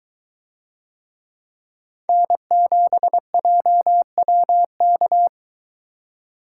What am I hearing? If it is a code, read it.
Morse code, heard as a keyed tone, about 23 wpm, 700 Hz. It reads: N7JWK